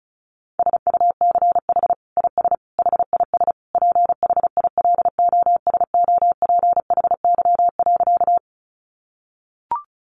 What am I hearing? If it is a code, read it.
Morse code, heard as a keyed tone, about 34 wpm, 700 Hz. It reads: SUCH IS HIS PHILOSOPHY.
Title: Sentences from Top 900 Words 35wpm